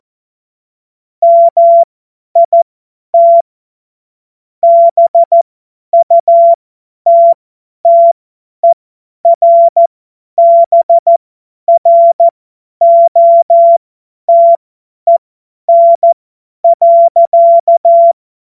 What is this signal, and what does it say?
Morse code, heard as a keyed tone, about 14 wpm, spaced as for 10 wpm, 680 Hz. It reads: MIT BUTTERBROTEN.
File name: Morsekurs / Witze / Witz 28. März 2023 50 F70